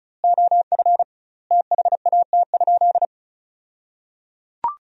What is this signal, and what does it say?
Morse code, heard as a keyed tone, about 36 wpm, 700 Hz. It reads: OF THAT?
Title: 2x Word-Spacing - Sentences from Top 100 Words - 35wpm